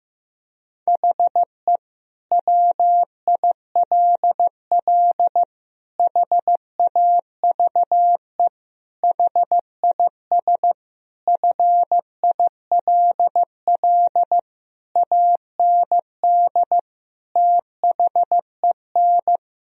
Morse code, heard as a keyed tone, about 15 wpm, 700 Hz. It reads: HE WILL HAVE HIS FILL AND THEN
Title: Sentences from Top 1000 Words - Review Part 3 - 15wpm